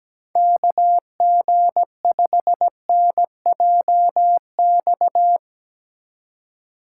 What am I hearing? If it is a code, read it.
Morse code, heard as a keyed tone, about 19 wpm, 700 Hz. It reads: KG5NJX
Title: Speed Racing - USA 2x3 Call Signs 17wpm